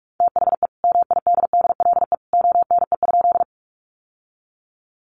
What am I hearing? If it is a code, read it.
Morse code, heard as a keyed tone, about 44 wpm, 700 Hz. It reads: THE MIDDLE ONE?
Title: Sentences from Top 900 Words 45wpm